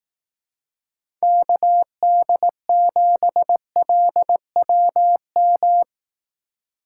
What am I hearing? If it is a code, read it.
Morse code, heard as a keyed tone, about 18 wpm, 700 Hz. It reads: KD7LWM
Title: Speed Racing - USA 2x3 Call Signs 15wpm